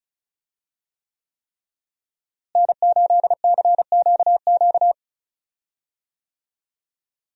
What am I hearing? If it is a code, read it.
Morse code, heard as a keyed tone, about 34 wpm, 700 Hz. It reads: N8CQQ
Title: USA Call Signs - 5 Characters - Repeated 3x - 35wpm